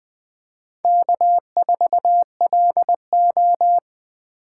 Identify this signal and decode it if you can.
Morse code, heard as a keyed tone, about 21 wpm, 700 Hz. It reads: K4LO
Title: USA Call Signs - 4 Characters - 2x Character Spacing - 20wpm